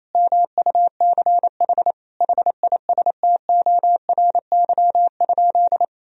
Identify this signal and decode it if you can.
Morse code, heard as a keyed tone, about 28 wpm, 700 Hz. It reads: MUCH HISTORY?
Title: Sentences from Top 1000 Words - Review Part 4 - 28wpm